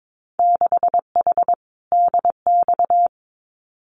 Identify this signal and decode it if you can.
Morse code, heard as a keyed tone, about 22 wpm, 700 Hz. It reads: 6H DX
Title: Winter Field Day Exchanges Encore 22wpm